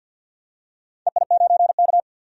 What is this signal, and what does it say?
Morse code, heard as a keyed tone, about 50 wpm, 700 Hz. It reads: EI9K
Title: DXCC-100 Call Signs - 4 Characters - 50wpm